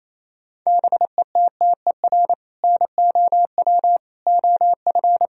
Morse code, heard as a keyed tone, about 28 wpm, 700 Hz. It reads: BETTER NOW OF
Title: Sentences from Top 1000 Words - Review Part 3 - 28wpm